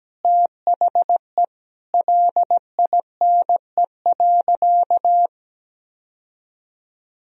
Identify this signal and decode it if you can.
Morse code, heard as a keyed tone, about 17 wpm, 700 Hz. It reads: THE LINE.
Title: Sentences from Top 1000 Words - Review Part 1 - 17wpm